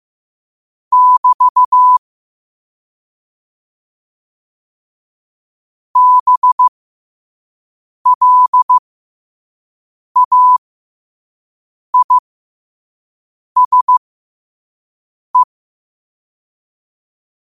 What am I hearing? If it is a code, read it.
Morse code, heard as a keyed tone, about 15 wpm, spaced as for 5 wpm, 1000 Hz. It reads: <BT> BLAISE
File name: qotdfiles / 5wpm